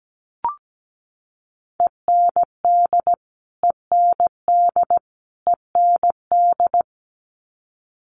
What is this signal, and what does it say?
Morse code, heard as a keyed tone, about 17 wpm, 700 Hz. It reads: END END END
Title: Top 500 Words Repeated 17wpm